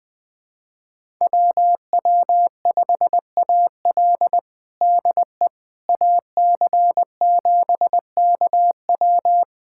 Morse code, heard as a keyed tone, about 20 wpm, 700 Hz. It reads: WW5AL DE AC7KW
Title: USA Callsign DE Callsign - 5 Characters - 20wpm